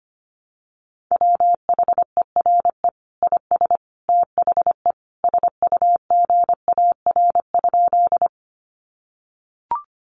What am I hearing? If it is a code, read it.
Morse code, heard as a keyed tone, about 25 wpm, 700 Hz. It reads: WHERE IS THE SUGAR?
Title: Sentences from Top 1200 Words 25wpm